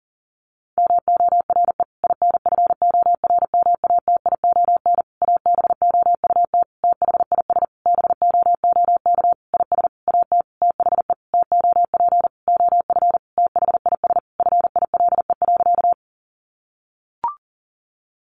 Morse code, heard as a keyed tone, about 40 wpm, 700 Hz. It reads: MORE INFORMATION ABOUT THIS BOOK IS AT THE TOP OF THIS FILE.
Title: Sentences from Top 1200 Words 40wpm